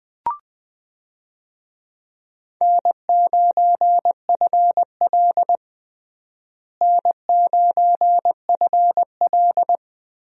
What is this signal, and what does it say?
Morse code, heard as a keyed tone, about 20 wpm, 700 Hz. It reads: N9FL N9FL
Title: USA Call Signs - 4 Characters - Repeated 3x - 20wpm